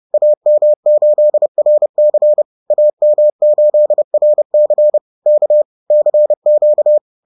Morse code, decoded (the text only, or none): AM8RC AM8RC K CQ